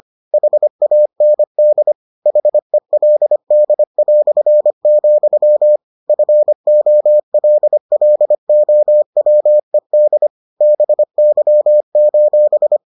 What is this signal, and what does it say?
Morse code, heard as a keyed tone, about 25 wpm, 600 Hz. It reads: HAND HELD", FOLLOWED BY: